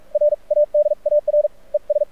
Morse code, heard as a keyed tone, about 44 wpm, spaced as for 37 wpm, 590 Hz. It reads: RADAR ES
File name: archived / CRSnet